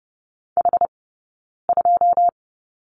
Morse code, heard as a keyed tone, about 30 wpm, 700 Hz. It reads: H 2
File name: course / 30:10wpm / 086 Lesson 15d - Number 5 - Two Characters 30wpm